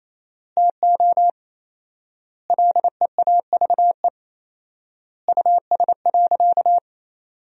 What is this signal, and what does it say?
Morse code, heard as a keyed tone, about 28 wpm, 700 Hz. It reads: TO LEAVE US.
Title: 4x Word-Spacing - Sentences from Top 400 Words - 28wpm